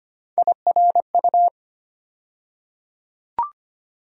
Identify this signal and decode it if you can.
Morse code, heard as a keyed tone, about 25 wpm, 700 Hz. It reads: IRU
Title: USA Call Sign Postfixes - 3 Letters 25wpm